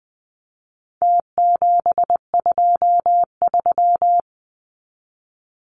Morse code, heard as a keyed tone, about 20 wpm, 700 Hz. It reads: T723
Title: Numbers - Contest Style - 4-Digit 20wpm